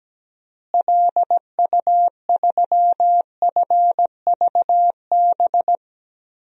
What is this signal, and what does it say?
Morse code, heard as a keyed tone, about 17 wpm, 700 Hz. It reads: LU3FVB
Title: DXCC-100 Call Signs - 6 Characters - Repeated 3x - 17wpm